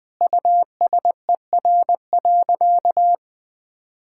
Morse code, heard as a keyed tone, about 20 wpm, 700 Hz. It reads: USER.
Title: Sentences from Top 1000 Words - Review Part 5 - 20wpm